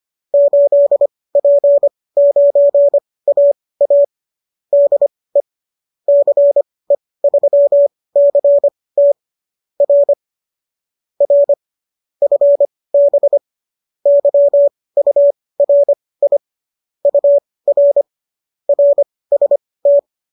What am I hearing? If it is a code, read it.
Morse code, heard as a keyed tone, about 25 wpm, spaced as for 18 wpm, 570 Hz. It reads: 8P9AA DE CE3CT R R FB YURI UR RST